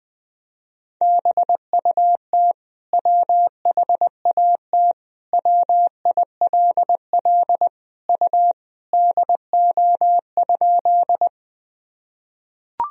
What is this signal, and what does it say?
Morse code, heard as a keyed tone, about 19 wpm, 700 Hz. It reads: BUT WHAT WILL U DO?
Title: Sentences from Top 100 Words and Abbreviations 20wpm